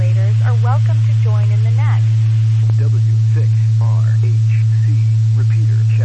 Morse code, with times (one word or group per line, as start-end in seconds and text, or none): none